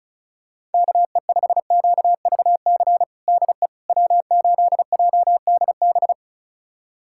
Morse code, heard as a keyed tone, about 34 wpm, 700 Hz. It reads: KE5QVC DE W8JDB
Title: USA Callsign DE Callsign 35wpm